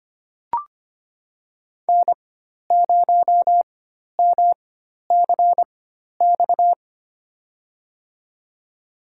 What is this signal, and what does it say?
Morse code, heard as a keyed tone, about 25 wpm, spaced as for 12 wpm, 700 Hz. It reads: N0MCX
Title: USA Call Signs - 5 Characters - 4x Character Spacing - 25wpm